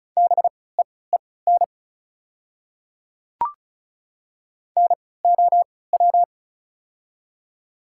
Morse code, heard as a keyed tone, about 36 wpm, spaced as for 20 wpm, 690 Hz. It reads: BEEN NOW
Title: 3x Character Spacing - Top 100 Words - 35wpm